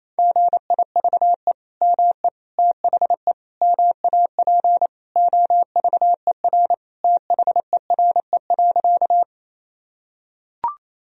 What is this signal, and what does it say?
Morse code, heard as a keyed tone, about 28 wpm, 700 Hz. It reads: GIVE ME THE MAP OVER THERE.
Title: Sentences from Top 1000 Words - Review Part 2 - 28wpm